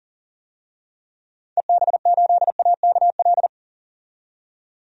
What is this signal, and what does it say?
Morse code, heard as a keyed tone, about 40 wpm, 700 Hz. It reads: EB8AKL
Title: DXCC-100 Call Signs - 6 Characters - 40wpm